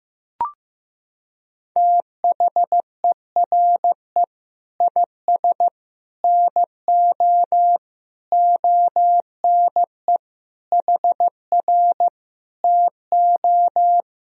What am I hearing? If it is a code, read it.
Morse code, heard as a keyed tone, about 15 wpm, 700 Hz. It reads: THERE IS NO ONE HR TO
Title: Sentences from Top 200 Words and Abbreviations 15wpm